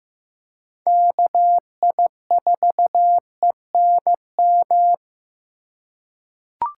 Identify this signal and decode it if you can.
Morse code, heard as a keyed tone, about 16 wpm, 700 Hz. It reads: KI4ENM
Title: USA Call Signs - 6 Characters - 2x Character Spacing - 15wpm